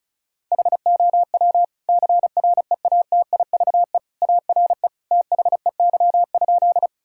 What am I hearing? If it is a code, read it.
Morse code, heard as a keyed tone, about 34 wpm, 700 Hz. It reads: HOW CREATIVE ARE THEY?
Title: Sentences from Top 1000 Words - Review Part 5 - 35wpm